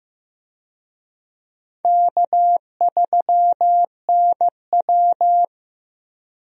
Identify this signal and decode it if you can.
Morse code, heard as a keyed tone, about 15 wpm, 700 Hz. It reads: K3NW
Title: USA Call Signs - 4 Characters - Repeated 2x - 15wpm